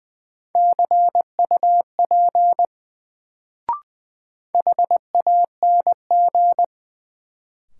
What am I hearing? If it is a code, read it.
Morse code, heard as a keyed tone, about 20 wpm, 700 Hz. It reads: CUP HANG
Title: Sentences from Top 1600 Words - New Word Drill - 20wpm